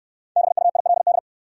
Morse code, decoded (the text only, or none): BUILD